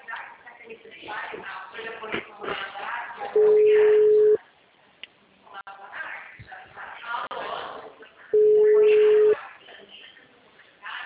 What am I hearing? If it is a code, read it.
Ringback tone (Europe, 425 Hz), with speech.
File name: out- -2017- - - .97182